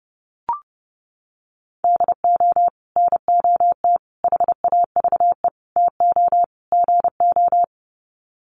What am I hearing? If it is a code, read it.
Morse code, heard as a keyed tone, about 30 wpm, 700 Hz. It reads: DO NOT HAVE TO GO
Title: Sets of 5 Words 30wpm